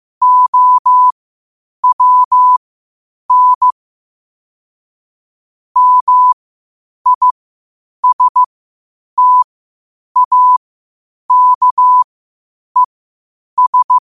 Morse code, heard as a keyed tone, about 15 wpm, spaced as for 8 wpm, 1000 Hz. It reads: OWN MISTAKES
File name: 8wpm